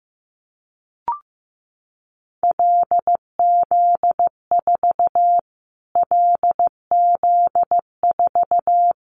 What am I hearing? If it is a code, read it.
Morse code, heard as a keyed tone, about 15 wpm, 700 Hz. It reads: LZ4 LZ4
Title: Callsign Prefixes for Top 25 DXCC - Repeated 3x - 15wpm